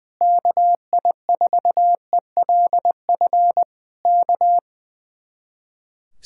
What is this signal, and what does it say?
Morse code, heard as a keyed tone, about 20 wpm, 700 Hz. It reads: KI4ELF K
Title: Speed-Racing Common QSO Phrases Encore - 20wpm